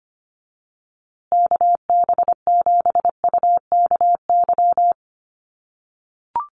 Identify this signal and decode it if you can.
Morse code, heard as a keyed tone, about 25 wpm, 700 Hz. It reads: KB7UKY
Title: USA Call Signs Repeated 2x 25wpm